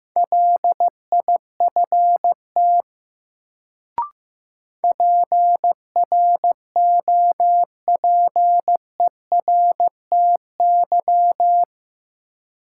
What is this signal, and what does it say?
Morse code, heard as a keyed tone, about 15 wpm, 700 Hz. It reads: LIFT PROPERTY
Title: Sentences from Top 1000 Words - New Word Drill - 15wpm